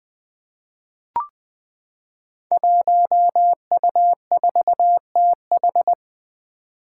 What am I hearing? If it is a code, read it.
Morse code, heard as a keyed tone, about 20 wpm, 700 Hz. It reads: 1U4TH
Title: Five Challenging Letter-Number Combinations 20wpm